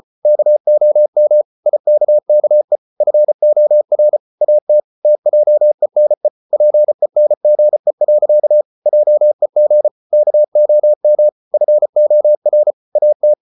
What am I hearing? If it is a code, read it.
Morse code, heard as a keyed tone, about 34 wpm, 600 Hz. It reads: KOM IKKE FOR AT TJENE PENGE. JEG KOM FOR AT
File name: Ferie QRQ 34wpm